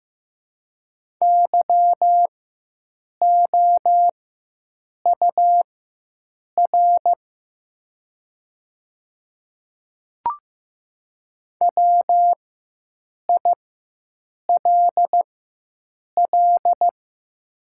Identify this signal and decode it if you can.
Morse code, heard as a keyed tone, about 15 wpm, spaced as for 7 wpm, 700 Hz. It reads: YOUR WILL
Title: 4x Character Spacing - Top 100 Words - 15wpm